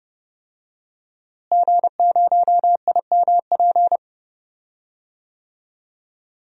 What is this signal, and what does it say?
Morse code, heard as a keyed tone, about 32 wpm, 700 Hz. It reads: G0IMP
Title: DXCC-100 Call Signs - 5 Characters - Repeated 2x - 30wpm